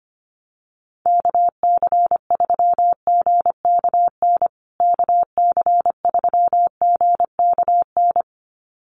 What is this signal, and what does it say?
Morse code, heard as a keyed tone, about 25 wpm, 700 Hz. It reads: KC3GKN KC3GKN
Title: USA Call Signs Repeated 2x 25wpm